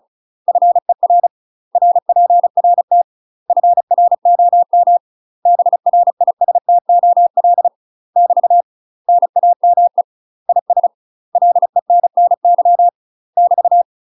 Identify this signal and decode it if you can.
Morse code, heard as a keyed tone, about 36 wpm, 700 Hz. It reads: FER RPRT FROM BRISTOL <BT> NAME IS LENNY <BT>
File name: UK MASTER QSOs TRAINING / 35WPM / UK MASTER QSOs 0036